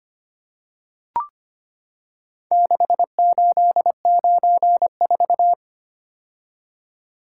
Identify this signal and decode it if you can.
Morse code, heard as a keyed tone, about 25 wpm, 700 Hz. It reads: 6894
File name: practice / 25wpm / Four Letter-Number Combinations - 25wpm